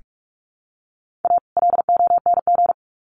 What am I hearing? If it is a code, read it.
Morse code, heard as a keyed tone, about 44 wpm, 700 Hz. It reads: A LONG
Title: Sets of 2 Words 45wpm